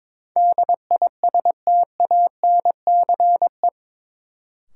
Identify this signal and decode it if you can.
Morse code, heard as a keyed tone, about 22 wpm, 700 Hz. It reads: DISTANCE
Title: Sentences from Top 1200 Words - New Word Drill Alt - 22wpm